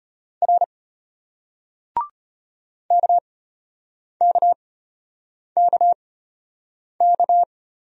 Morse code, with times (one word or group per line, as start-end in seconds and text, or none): 0.4-0.6 R
2.9-7.4 KKKK